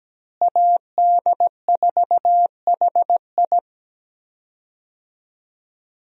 Morse code, heard as a keyed tone, about 17 wpm, 700 Hz. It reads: AD4HI
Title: USA Call Signs - 5 Characters - Repeated 3x - 17wpm